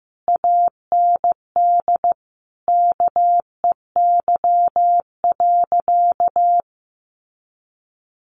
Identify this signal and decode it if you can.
Morse code, heard as a keyed tone, about 15 wpm, 700 Hz. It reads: AND KEY.
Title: Sentences from Top 1400 Words 15wpm